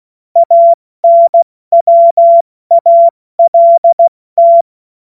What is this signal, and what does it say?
Morse code, heard as a keyed tone, about 16 wpm, spaced as for 14 wpm, 680 Hz. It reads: ANWALT
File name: Morsekurs / Witze / Witz 31. Mai 2022 70 F80